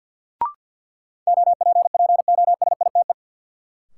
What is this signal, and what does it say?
Morse code, heard as a keyed tone, about 52 wpm, 700 Hz. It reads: OPPOSITE
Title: Sentences from Top 1300 Words - New Word Drill - 50wpm